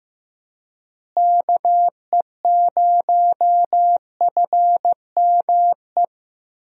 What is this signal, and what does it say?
Morse code, heard as a keyed tone, about 15 wpm, 700 Hz. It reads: KE0FME
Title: USA Call Signs - 6 Characters - Repeated 3x - 15wpm